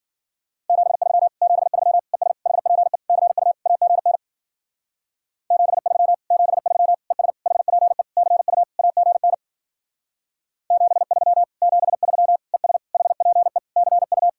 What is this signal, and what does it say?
Morse code, heard as a keyed tone, about 50 wpm, 700 Hz. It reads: 73 73 ES HPE CU AGN 73 73 ES HPE CU AGN 73 73 ES HPE CU
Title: Speed-Racing Common QSO Phrases Encore - 40wpm